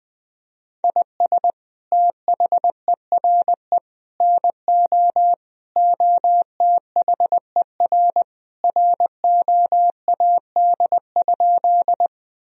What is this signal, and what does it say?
Morse code, heard as a keyed tone, about 20 wpm, 700 Hz. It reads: IS THERE NO OTHER ROAD?
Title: Sentences from Top 1000 Words - Review Part 2 - 20wpm